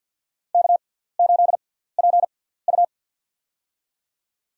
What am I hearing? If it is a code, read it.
Morse code, heard as a keyed tone, about 53 wpm, spaced as for 17 wpm, 700 Hz. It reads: K8PU
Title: USA Call Signs - 4 Characters - 6x Character Spacing - 50wpm